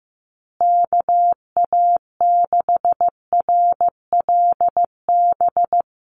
Morse code, heard as a keyed tone, about 15 wpm, 700 Hz. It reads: KA6RLB
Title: USA Call Signs Repeated 3x 15wpm